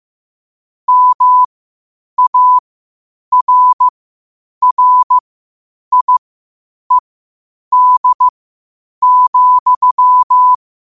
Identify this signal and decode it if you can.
Morse code, heard as a keyed tone, about 15 wpm, spaced as for 8 wpm, 1000 Hz. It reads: MARRIED,
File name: qotdfiles / 8wpm